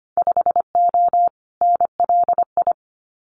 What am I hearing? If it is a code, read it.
Morse code, heard as a keyed tone, about 25 wpm, 700 Hz. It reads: 5O NLI
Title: Winter Field Day Exchanges 25wpm